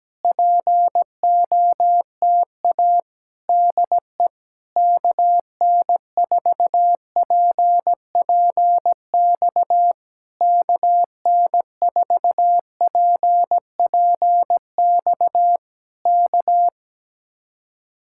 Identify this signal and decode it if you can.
Morse code, heard as a keyed tone, about 17 wpm, 700 Hz. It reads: POTA DE KN4PPX KN4PPX K
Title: Speed-Racing Common QSO Phrases - 17wpm